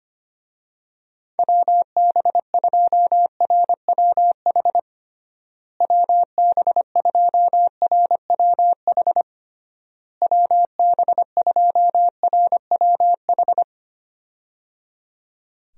Morse code, heard as a keyed tone, about 25 wpm, 700 Hz. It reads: WB2RWH WB2RWH WB2RWH
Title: USA Call Signs - 6 Characters - Repeated 3x - 25wpm